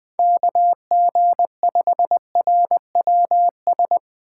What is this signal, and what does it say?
Morse code, heard as a keyed tone, about 21 wpm, 700 Hz. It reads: KG5RWS